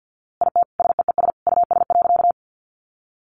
Morse code, heard as a keyed tone, about 50 wpm, 700 Hz. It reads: IT SEES US.
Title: A Sound Of Thunder 50wpm